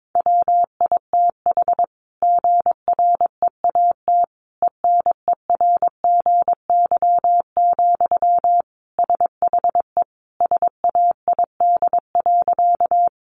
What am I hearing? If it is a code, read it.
Morse code, heard as a keyed tone, about 22 wpm, 700 Hz. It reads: WITH GREAT ENERGY, SHE SAID.
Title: Sentences from Top 900 Words 22wpm